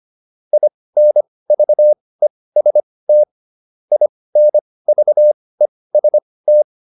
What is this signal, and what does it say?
Morse code, heard as a keyed tone, about 25 wpm, spaced as for 18 wpm, 600 Hz. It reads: INVEST INVEST